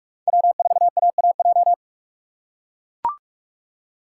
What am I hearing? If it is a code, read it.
Morse code, heard as a keyed tone, about 44 wpm, 700 Hz. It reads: W4AAJ